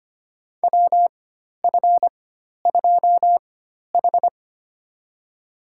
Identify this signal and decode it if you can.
Morse code, heard as a keyed tone, about 25 wpm, spaced as for 12 wpm, 700 Hz. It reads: WF2H
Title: USA Call Signs - 4 Characters - 4x Character Spacing - 25wpm